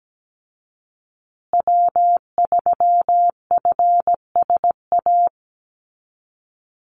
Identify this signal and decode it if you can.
Morse code, heard as a keyed tone, about 17 wpm, 700 Hz. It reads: W3FSA
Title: Call Signs Repeated 2x - 17wpm